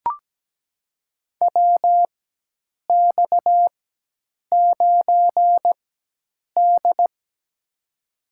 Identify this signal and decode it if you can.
Morse code, heard as a keyed tone, about 17 wpm, spaced as for 8 wpm, 700 Hz. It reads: WX9D
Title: USA Call Signs - 4 Characters - 4x Character Spacing - 17wpm